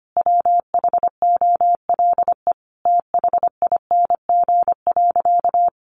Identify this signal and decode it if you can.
Morse code, heard as a keyed tone, about 25 wpm, 700 Hz. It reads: WHOLE THING.
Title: Sentences from Top 400 Words 25wpm